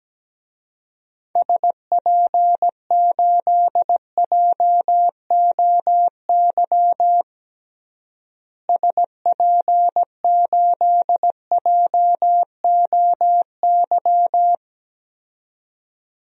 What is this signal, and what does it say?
Morse code, heard as a keyed tone, about 17 wpm, 700 Hz. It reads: SP8JOY SP8JOY
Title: DXCC-100 Call Signs - 6 Characters - Repeated 2x - 17wpm